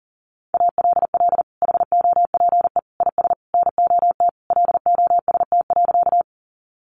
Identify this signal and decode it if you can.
Morse code, heard as a keyed tone, about 37 wpm, 700 Hz. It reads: ALL HOPE IS NOT LOST.
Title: Sentences from Top 1400 Words 40wpm